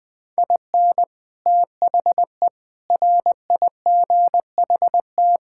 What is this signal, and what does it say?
Morse code, heard as a keyed tone, about 20 wpm, 700 Hz. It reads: IN THE RIGHT
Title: Sets of 8 Words - 20wpm